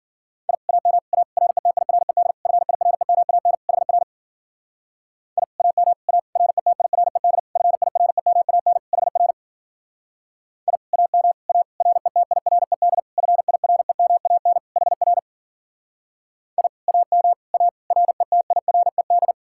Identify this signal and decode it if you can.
Morse code, heard as a keyed tone, about 50 wpm, 700 Hz. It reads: I AM A RETIRED FIREMAN HR I AM A RETIRED FIREMAN HR I AM A RETIRED FIREMAN HR I AM A RETIRED
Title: Speed-Racing Common QSO Phrases - 40wpm